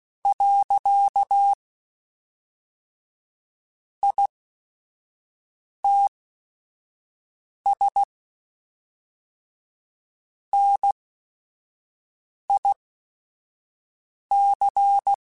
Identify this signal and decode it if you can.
Morse code, heard as a keyed tone, about 16 wpm, spaced as for 5 wpm, 800 Hz. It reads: . ITS NIC
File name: QSOs / QSOs-mp3s / WD0BGP 2@5